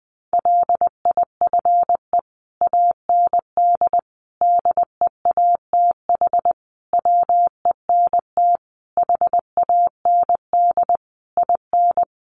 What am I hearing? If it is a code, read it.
Morse code, heard as a keyed tone, about 20 wpm, 700 Hz. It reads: LIFE AND DEATH WENT HAND IN
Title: Sentences from Top 1100 Words 20wpm